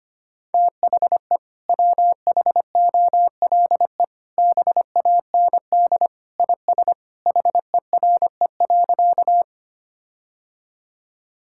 Morse code, heard as a keyed tone, about 25 wpm, 700 Hz. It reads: THE WHOLE BAND IS HERE.
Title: Sentences from Top 1600 Words 25wpm